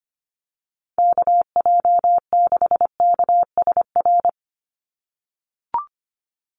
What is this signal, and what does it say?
Morse code, heard as a keyed tone, about 25 wpm, 700 Hz. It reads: KJ6KSR
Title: USA Call Signs 25wpm